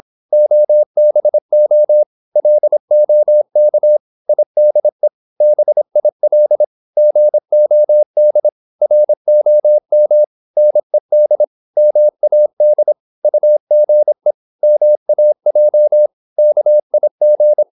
Morse code, heard as a keyed tone, about 26 wpm, 600 Hz. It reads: OBO LOK IDE BIL GOD ROM NED MAD UGE MAJ KIG
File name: Korte ord DK 26wpm